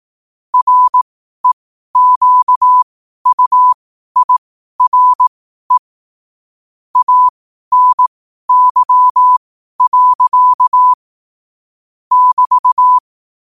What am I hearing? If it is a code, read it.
Morse code, heard as a keyed tone, about 18 wpm, spaced as for 12 wpm, 1000 Hz. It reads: REQUIRE ANY. <BT>
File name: qotdfiles / 12wpm